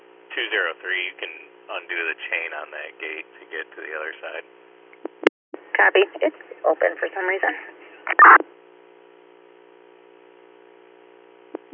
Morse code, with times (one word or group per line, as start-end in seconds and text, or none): none